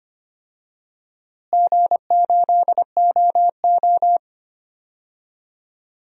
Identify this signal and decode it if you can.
Morse code, heard as a keyed tone, about 25 wpm, 700 Hz. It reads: G8OO
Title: DXCC-100 Call Signs - 4 Characters - Repeated 2x - 25wpm